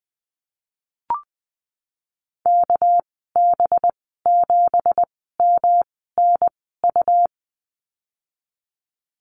Morse code, heard as a keyed tone, about 19 wpm, spaced as for 14 wpm, 700 Hz. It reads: KB7MNU
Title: 207 Lesson 33g - Letter j - Call Signs 20wpm